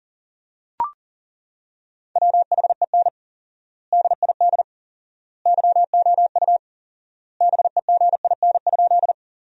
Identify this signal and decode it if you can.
Morse code, heard as a keyed tone, about 40 wpm, 700 Hz. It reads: WHEN DID YOU BEGIN?
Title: 4x Word-Spacing - Sentences from Top 300 Words - 40wpm